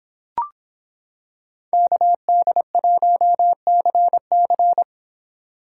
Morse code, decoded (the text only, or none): KD1CC